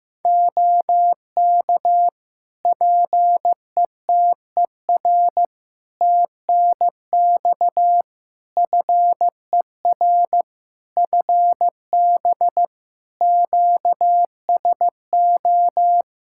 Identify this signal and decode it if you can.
Morse code, heard as a keyed tone, about 16 wpm, 700 Hz. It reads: OK PETER TNX FER FB QSO
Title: Practice QSO - Exchange 4 - Ending - 15wpm